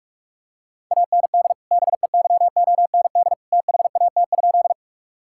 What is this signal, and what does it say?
Morse code, heard as a keyed tone, about 46 wpm, 700 Hz. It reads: AND BEYOND THAT?
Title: Sentences from Top 1000 Words - Review Part 6 - 45wpm